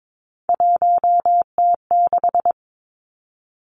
Morse code, heard as a keyed tone, about 22 wpm, 700 Hz. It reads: 1T6
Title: Three Challenging Letter-Number Combinations 22wpm